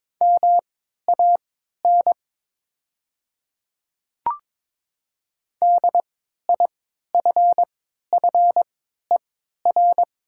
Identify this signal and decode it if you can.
Morse code, heard as a keyed tone, about 22 wpm, spaced as for 12 wpm, 700 Hz. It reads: MAN DIFFER
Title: 3x Character Spacing - Top 200 Words - 22wpm